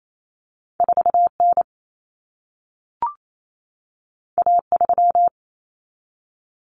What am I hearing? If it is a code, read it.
Morse code, heard as a keyed tone, about 28 wpm, 700 Hz. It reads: 4N A3
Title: Common Cut Numbers - 2-Digit 28wpm